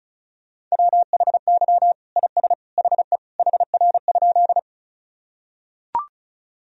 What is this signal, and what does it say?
Morse code, heard as a keyed tone, about 36 wpm, 700 Hz. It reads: WHY IS HE HR?
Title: Sentences from Top 200 Words and Abbreviations 35wpm